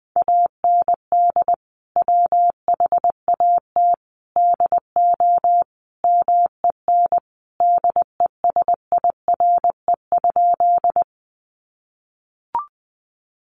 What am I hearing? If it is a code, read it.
Morse code, heard as a keyed tone, about 20 wpm, 700 Hz. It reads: AND WHAT DO MEN DESIRE?
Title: Sentences from Top 1100 Words 20wpm